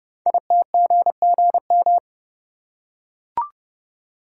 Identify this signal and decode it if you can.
Morse code, heard as a keyed tone, about 30 wpm, 700 Hz. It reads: ITGGM
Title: Code Groups - 5 Letters - 30wpm